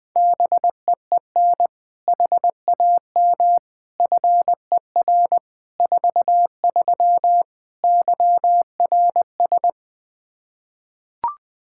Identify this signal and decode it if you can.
Morse code, heard as a keyed tone, about 21 wpm, 700 Hz. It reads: BEEN HAM FER 43 YRS
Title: Practice QSO - Exchange 3 - More Info - 20wpm